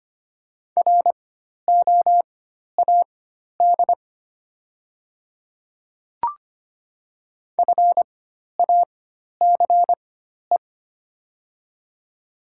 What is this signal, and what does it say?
Morse code, heard as a keyed tone, about 25 wpm, spaced as for 12 wpm, 700 Hz. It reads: ROAD FACE
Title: 4x Character Spacing - Top 500 Words - 25wpm